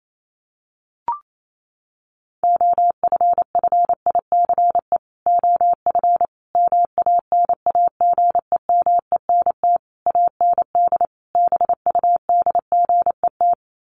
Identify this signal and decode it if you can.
Morse code, heard as a keyed tone, about 28 wpm, 700 Hz. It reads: OFFICE OF MANAGEMENT AND BUDGET
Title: Sets of 5 Words Encore2 28wpm